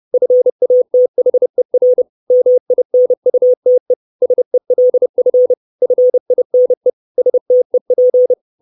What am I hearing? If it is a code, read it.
Morse code, heard as a keyed tone, about 30 wpm, 500 Hz. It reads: FATHER MINUTE SELF FINE STEP
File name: 632 Words 30WPM